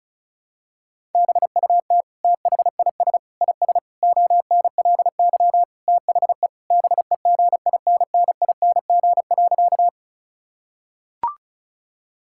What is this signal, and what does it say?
Morse code, heard as a keyed tone, about 34 wpm, 700 Hz. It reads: BUT THIS IS ONLY THE BEGINNING.
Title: Sentences from Top 1000 Words - Review Part 5 - 35wpm